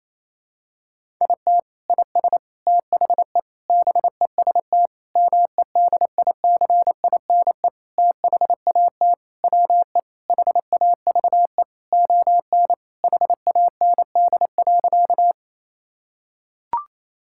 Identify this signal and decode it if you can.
Morse code, heard as a keyed tone, about 28 wpm, 700 Hz. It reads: IT IS THE BEST MEDICINE THAT WE HAVE ON HAND.
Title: Sentences from Top 1000 Words - Review Part 6 - 28wpm